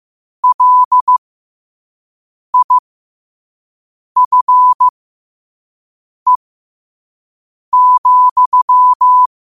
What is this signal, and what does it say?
Morse code, heard as a keyed tone, about 15 wpm, spaced as for 5 wpm, 1000 Hz. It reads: LIFE,
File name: qotdfiles / 5wpm